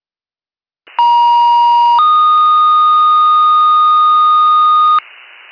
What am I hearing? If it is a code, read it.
Two-tone page, heard as a one-second tone, then a three-second tone, about 950 then 1230 Hz. It is heard from 1.0 to 5.0 s.